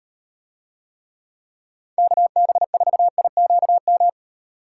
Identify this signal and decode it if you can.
Morse code, heard as a keyed tone, about 38 wpm, 700 Hz. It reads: KB4IQM